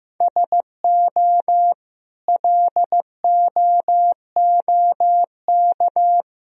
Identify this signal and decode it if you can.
Morse code, heard as a keyed tone, about 16 wpm, 700 Hz. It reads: SO LOOK
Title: Sentences from Top 1000 Words - Review Part 1 - 15wpm